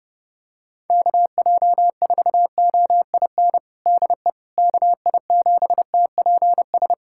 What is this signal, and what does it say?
Morse code, heard as a keyed tone, about 30 wpm, 700 Hz. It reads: KJ4OIN DE KI7TPS
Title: USA Callsign DE Callsign 30wpm